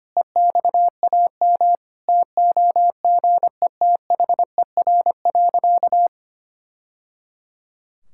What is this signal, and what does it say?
Morse code, heard as a keyed tone, about 25 wpm, 700 Hz. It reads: EXAM TOGETHER.
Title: Sentences from Top 1000 Words - Review Part 5 - 25wpm